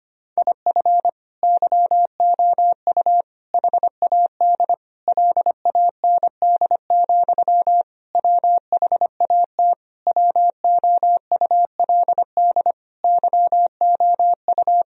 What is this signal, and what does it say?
Morse code, heard as a keyed tone, about 25 wpm, 700 Hz. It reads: IF YOU HAD LAND, WHAT WOULD YOU
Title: Sentences from Top 1000 Words - Review Part 1 - 25wpm